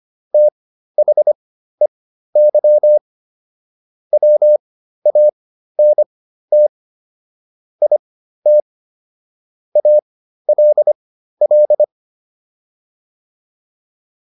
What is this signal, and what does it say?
Morse code, heard as a keyed tone, about 25 wpm, spaced as for 13 wpm, 600 Hz. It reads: THEY WANT IT ALL